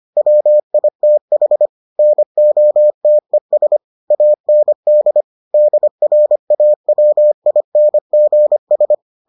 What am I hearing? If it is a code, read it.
Morse code, heard as a keyed tone, about 25 wpm, 600 Hz. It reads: WITH NOTES AND DRAWINGS